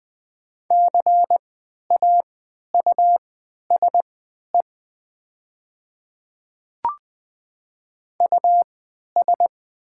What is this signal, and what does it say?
Morse code, heard as a keyed tone, about 20 wpm, spaced as for 11 wpm, 700 Hz. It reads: CAUSE US
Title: 3x Character Spacing - Top 200 Words - 20wpm